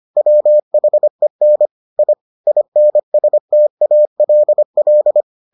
Morse code, heard as a keyed tone, about 25 wpm, 600 Hz. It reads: WHEN I INSTALL